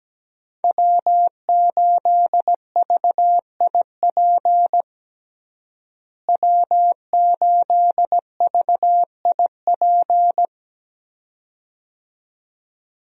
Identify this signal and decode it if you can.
Morse code, heard as a keyed tone, about 17 wpm, 700 Hz. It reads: W8VIP W8VIP
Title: USA Call Signs - 5 Characters - Repeated 2x - 17wpm